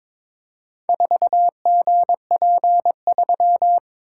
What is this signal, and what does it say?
Morse code, heard as a keyed tone, about 22 wpm, 700 Hz. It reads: 4GP3
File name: practice / 22wpm / Four Letter-Number Combinations - 22wpm